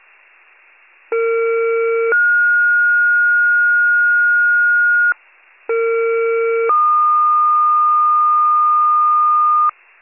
Two-tone page, heard as a one-second tone, then a three-second tone, about 460 then 1430 Hz, about 460 then 1220 Hz, with noise. It is heard from 1.1 to 9.7 s.